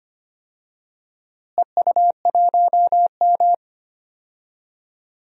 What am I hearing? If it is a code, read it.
Morse code, heard as a keyed tone, about 25 wpm, 700 Hz. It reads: EU1M
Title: DXCC-100 Call Signs - 4 Characters - Repeated 2x - 25wpm